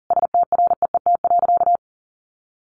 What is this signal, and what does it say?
Morse code, heard as a keyed tone, about 40 wpm, 700 Hz. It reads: STREET.
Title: Sentences from Top 1200 Words 40wpm